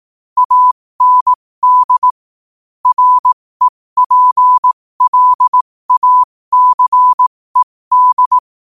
Morse code, heard as a keyed tone, about 18 wpm, spaced as for 15 wpm, 1000 Hz. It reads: AND REPLACED
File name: qotdfiles / 15wpm